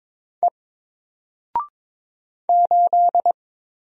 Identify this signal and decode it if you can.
Morse code, heard as a keyed tone, about 22 wpm, 700 Hz. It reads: E 8
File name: Single-Letter-Number-22wpm